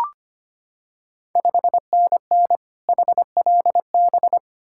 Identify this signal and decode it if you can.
Morse code, heard as a keyed tone, about 25 wpm, 690 Hz. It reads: 5NN HLB
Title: Ontario QSO Party Exchange 25wpm